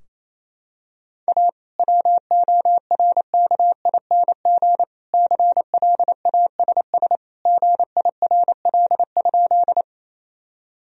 Morse code, heard as a keyed tone, about 28 wpm, 700 Hz. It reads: A WORKING CLASS GIRL?
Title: Sentences from Top 1000 Words - Review Part 4 - 28wpm